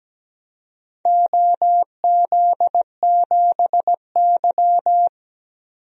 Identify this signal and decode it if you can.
Morse code, heard as a keyed tone, about 17 wpm, 700 Hz. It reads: OZ7Y
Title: DXCC-100 Call Signs - 4 Characters - 17wpm